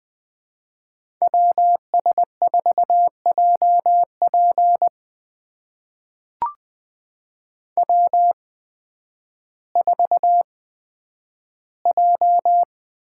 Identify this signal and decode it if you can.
Morse code, heard as a keyed tone, about 20 wpm, spaced as for 11 wpm, 700 Hz. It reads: WS4JP W4J
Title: USA Call Signs - 5 Characters - 8x Character Spacing - 20wpm